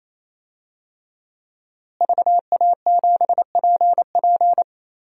Morse code, heard as a keyed tone, about 28 wpm, 700 Hz. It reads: VA7PP
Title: DXCC-100 Call Signs - 5 Characters - Repeated 3x - 28wpm